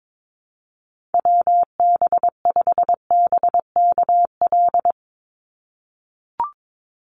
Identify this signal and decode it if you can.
Morse code, heard as a keyed tone, about 22 wpm, 700 Hz. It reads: WB5BKL
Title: NA SOTA Chaser Callsigns 22wpm